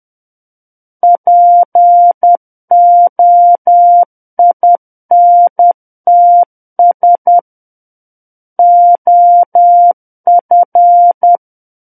Morse code, heard as a keyed tone, about 10 wpm, 700 Hz. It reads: POINTS OF
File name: Book One / 01-Chapter 1 - 10 WPM